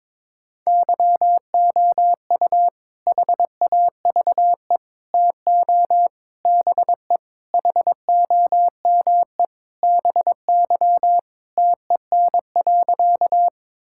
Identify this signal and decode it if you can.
Morse code, heard as a keyed tone, about 22 wpm, 700 Hz. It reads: YOU HAVE TO BE HOME BY TEN.
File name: practice / 22wpm / Sentences with 4-Word Phrases - 22wpm